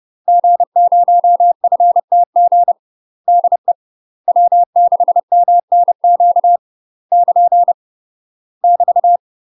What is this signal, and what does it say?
Morse code, heard as a keyed tone, about 30 wpm, 700 Hz. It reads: G0FTG DE W6MNQ <KN> <BT>
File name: UK MASTER QSOs TRAINING / 30WPM / UK MASTER QSOs 0014